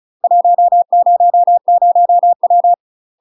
Morse code, heard as a keyed tone, about 34 wpm, 700 Hz. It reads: 100W
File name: UK MASTER QSOs TRAINING/35WPM